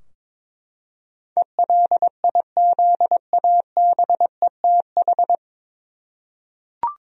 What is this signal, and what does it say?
Morse code, heard as a keyed tone, about 22 wpm, 700 Hz. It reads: ELIZABETH
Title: USA Common Names - 22wpm